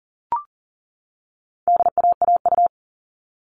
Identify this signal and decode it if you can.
Morse code, heard as a keyed tone, about 40 wpm, 700 Hz. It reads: DAAU
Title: All Cut Numbers - 4-Digit 40wpm